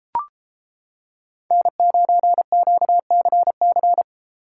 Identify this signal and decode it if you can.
Morse code, heard as a keyed tone, about 30 wpm, 700 Hz. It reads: N9QCC